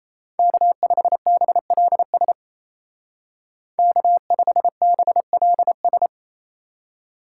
Morse code, heard as a keyed tone, about 34 wpm, 700 Hz. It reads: K5BLS K5BLS
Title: Speed Racing - USA 1x3 Call Signs 28wpm